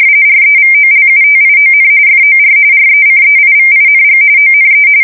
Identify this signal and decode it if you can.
RTTY, about 45.45 baud. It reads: REVOLUTION FOR RECEIVING ONE
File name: fsk170 / printing-telegraph